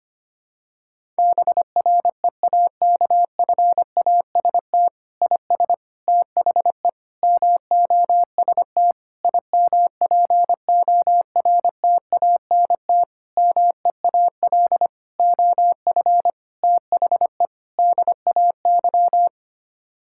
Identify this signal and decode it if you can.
Morse code, heard as a keyed tone, about 25 wpm, 700 Hz. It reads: BREAKFAST IS THE MOST IMPORTANT MEAL OF THE DAY
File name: Sets of 9 Words - 25wpm